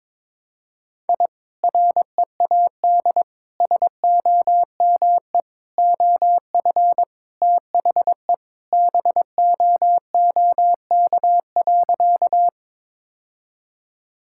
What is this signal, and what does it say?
Morse code, heard as a keyed tone, about 21 wpm, 700 Hz. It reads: I READ SOME OF THE BOOK.
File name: practice / 22wpm / Sentences with 2-Word Phrases - 22wpm